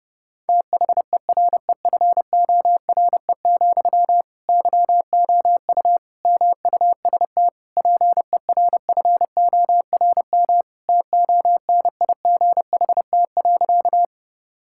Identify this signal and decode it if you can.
Morse code, heard as a keyed tone, about 31 wpm, 700 Hz. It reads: THEREFORE, YOU MUST PERFORM TONIGHT.
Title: Sentences from Top 1000 Words - Review Part 5 - 30wpm